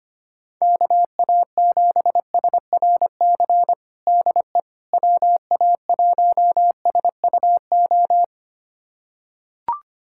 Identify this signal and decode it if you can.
Morse code, heard as a keyed tone, about 25 wpm, 700 Hz. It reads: KA7SRC DE WA1SUO
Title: USA Callsign DE Callsign - 6 Characters - 25wpm